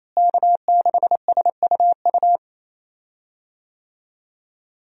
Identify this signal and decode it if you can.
Morse code, heard as a keyed tone, about 27 wpm, 700 Hz. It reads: K6SUU
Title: USA Call Signs - 5 Characters - Repeated 3x - 28wpm